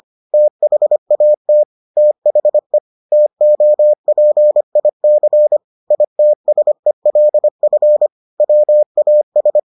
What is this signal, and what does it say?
Morse code, heard as a keyed tone, about 25 wpm, 600 Hz. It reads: THAT THE TOPIC ITSELF WAS